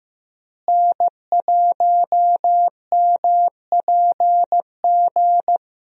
Morse code, heard as a keyed tone, about 15 wpm, 700 Hz. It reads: N1MPG
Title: USA Call Signs - 5 Characters - Repeated 2x - 15wpm